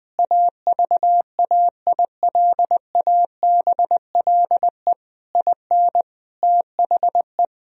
Morse code, heard as a keyed tone, about 20 wpm, 700 Hz. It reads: AVAILABLE IN THE
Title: Sentences from Top 1000 Words - Review Part 3 - 20wpm